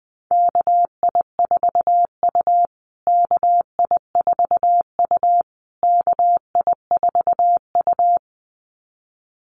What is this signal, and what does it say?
Morse code, heard as a keyed tone, about 20 wpm, 700 Hz. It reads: KI4U KI4U KI4U
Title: USA Call Signs Repeated 3x 20wpm